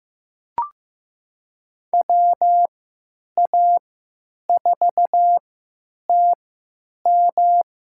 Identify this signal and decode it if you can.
Morse code, heard as a keyed tone, about 15 wpm, spaced as for 9 wpm, 700 Hz. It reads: WA4TM
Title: USA Call Signs - 3x Character Spacing - 15wpm